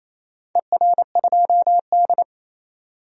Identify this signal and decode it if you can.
Morse code, heard as a keyed tone, about 28 wpm, 700 Hz. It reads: ER2D
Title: DXCC-100 Call Signs - 4 Characters - 28wpm